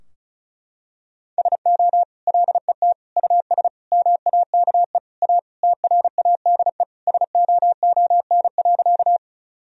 Morse code, heard as a keyed tone, about 36 wpm, 700 Hz. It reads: SO LET US MAKE A TRADE SOON.
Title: Sentences from Top 1000 Words - Review Part 4 - 35wpm